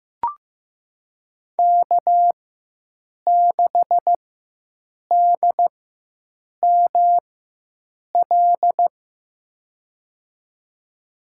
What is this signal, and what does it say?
Morse code, heard as a keyed tone, about 15 wpm, spaced as for 7 wpm, 700 Hz. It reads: K6DML
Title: USA Call Signs - 5 Characters - 4x Character Spacing - 15wpm